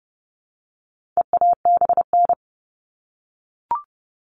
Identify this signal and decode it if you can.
Morse code, heard as a keyed tone, about 30 wpm, 700 Hz. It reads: EABN
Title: All Cut Numbers - 4-Digit 30wpm